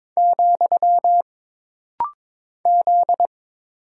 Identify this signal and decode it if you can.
Morse code, heard as a keyed tone, about 22 wpm, 700 Hz. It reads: , Z
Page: This rapid-fire timing is geared to push your ICR (Instant Character Recognition) skills.